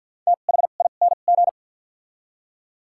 Morse code, heard as a keyed tone, about 50 wpm, spaced as for 36 wpm, 700 Hz. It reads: THING
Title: 2x Character Spacing - Top 500 Words - 50wpm